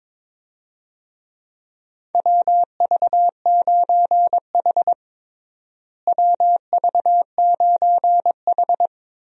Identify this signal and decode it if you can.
Morse code, heard as a keyed tone, about 21 wpm, 700 Hz. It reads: WV9H WV9H
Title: USA Call Signs - 4 Characters - Repeated 3x - 22wpm